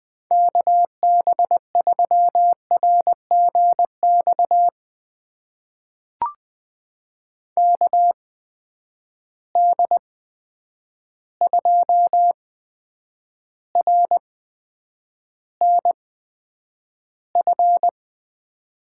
Morse code, heard as a keyed tone, about 20 wpm, spaced as for 9 wpm, 700 Hz. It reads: KB3RGX KD2RNF
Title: USA Call Signs - 6 Characters - 8x Character Spacing - 20wpm